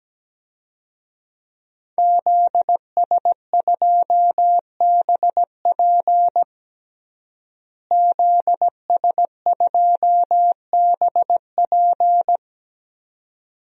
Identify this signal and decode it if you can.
Morse code, heard as a keyed tone, about 17 wpm, 700 Hz. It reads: ZS2BP ZS2BP
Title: DXCC-100 Call Signs - 5 Characters - Repeated 2x - 17wpm